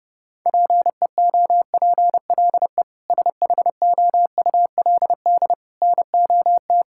Morse code, heard as a keyed tone, about 32 wpm, 700 Hz. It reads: PEOPLE SHOULD NOT
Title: Sentences from Top 1000 Words - Review Part 6 - 30wpm